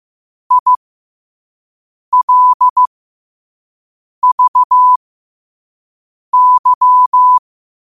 Morse code, heard as a keyed tone, about 15 wpm, spaced as for 5 wpm, 1000 Hz. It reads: ILVY
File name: qotdfiles / 5wpm